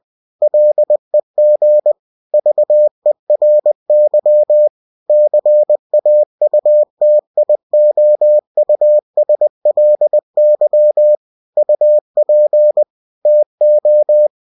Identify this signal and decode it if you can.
Morse code, heard as a keyed tone, about 20 wpm, 600 Hz. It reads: LEG VERY CAUTIOUSLY UP TO
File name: cw-books / winnie-the-pooh / winnie-20wpm-page-0054